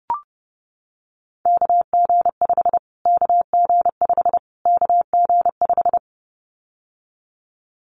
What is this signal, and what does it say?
Morse code, heard as a keyed tone, about 30 wpm, 700 Hz. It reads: KG5 KG5 KG5
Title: Callsign Prefixes for Top 25 DXCC - Repeated 3x - 30wpm